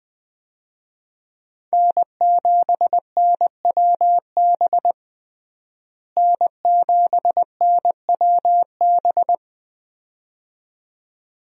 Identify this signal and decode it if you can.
Morse code, heard as a keyed tone, about 20 wpm, 700 Hz. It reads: N7NWB N7NWB
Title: USA Call Signs - 5 Characters - Repeated 2x - 20wpm